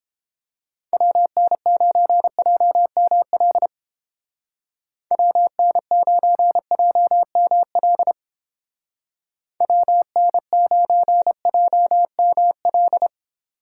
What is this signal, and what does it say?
Morse code, heard as a keyed tone, about 29 wpm, 700 Hz. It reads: WN9JML WN9JML WN9JML